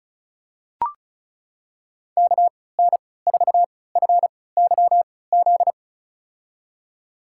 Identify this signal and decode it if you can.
Morse code, heard as a keyed tone, about 34 wpm, spaced as for 20 wpm, 690 Hz. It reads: KN4FYZ
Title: USA Call Signs - 6 Characters - 3x Character Spacing - 35wpm